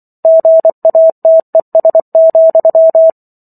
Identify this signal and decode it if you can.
Morse code, heard as a keyed tone, about 24 wpm, 650 Hz. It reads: GATES,